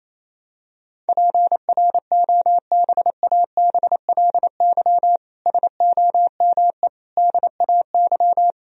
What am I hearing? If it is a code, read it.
Morse code, heard as a keyed tone, about 28 wpm, 700 Hz. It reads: PROBABLY SOME DAY
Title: Sentences from Top 1000 Words - Review Part 4 - 28wpm